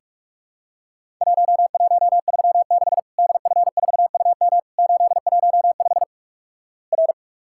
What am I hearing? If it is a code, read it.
Morse code, heard as a keyed tone, about 48 wpm, 700 Hz. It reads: 1136 DF4UM 815 R
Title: Worked All Europe DX Contest - QTC Exchange - 45wpm